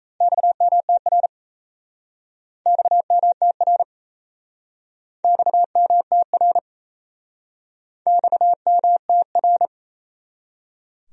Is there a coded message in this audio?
Yes: Morse code from 0.2 to 9.7 s, heard as a keyed tone, about 38 wpm, 700 Hz. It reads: XMTR XMTR XMTR XMTR